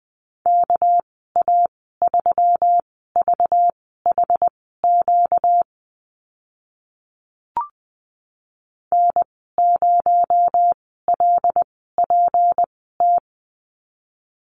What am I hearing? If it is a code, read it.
Morse code, heard as a keyed tone, about 20 wpm, spaced as for 15 wpm, 700 Hz. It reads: KA3VHQ N0LPT
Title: 245 Lesson 39g - Letter z - Call Signs 20wpm